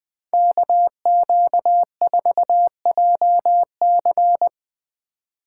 Morse code, heard as a keyed tone, about 20 wpm, 700 Hz. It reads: KQ4JC
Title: USA Call Signs - 5 Characters - 2x Character Spacing - 20wpm